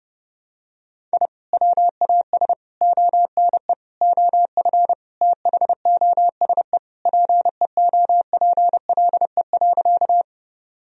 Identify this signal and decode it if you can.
Morse code, heard as a keyed tone, about 32 wpm, 700 Hz. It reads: I WAS ONE OF THOSE PEOPLE.
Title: Sentences from Top 1000 Words - Review Part 1 - 30wpm